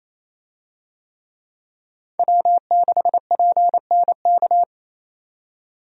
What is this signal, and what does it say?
Morse code, heard as a keyed tone, about 28 wpm, 700 Hz. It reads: W6PNK
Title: USA Call Signs - 5 Characters - 28wpm